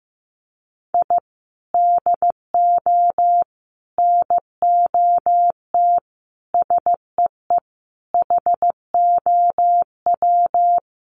Morse code, heard as a keyed tone, about 15 wpm, 700 Hz. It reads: I DO NOT SEE HOW
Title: Sets of 5 Words 15wpm